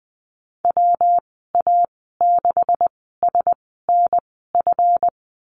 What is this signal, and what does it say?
Morse code, heard as a keyed tone, about 20 wpm, spaced as for 14 wpm, 700 Hz. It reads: WA6SNF
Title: 140 Lesson 23g - Letter f - Call Signs 20wpm